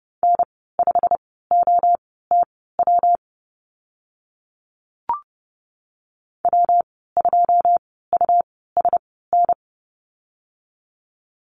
Morse code, heard as a keyed tone, about 30 wpm, spaced as for 17 wpm, 700 Hz. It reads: N5OTW W2USN
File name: Lesson 19g - Letter w - Call Signs 30wpm